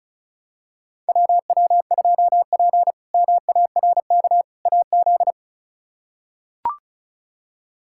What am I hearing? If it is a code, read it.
Morse code, heard as a keyed tone, about 36 wpm, 700 Hz. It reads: WW2P MARK AZ
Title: K1USN SST QSO Practice - 35wpm